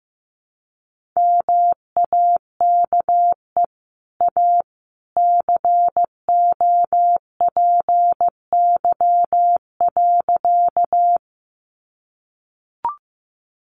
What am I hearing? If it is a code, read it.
Morse code, heard as a keyed tone, about 15 wpm, 700 Hz. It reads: MAKE A COPY.
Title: Sentences from Top 1400 Words 15wpm